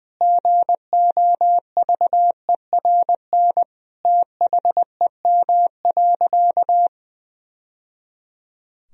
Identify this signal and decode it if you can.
Morse code, heard as a keyed tone, about 20 wpm, 700 Hz. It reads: GOVERN THEM.
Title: Sentences from Top 1000 Words - Review Part 2 - 20wpm